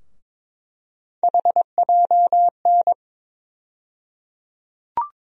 Morse code, heard as a keyed tone, about 22 wpm, 700 Hz. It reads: HJN
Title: USA Call Sign Postfixes - 3 Letters 22wpm